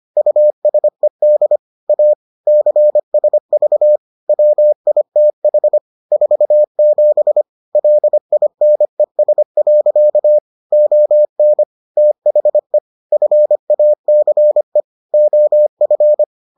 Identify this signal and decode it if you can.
Morse code, heard as a keyed tone, about 25 wpm, 600 Hz. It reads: USED A CSV WITH 47 LINES. ON THE FACE OF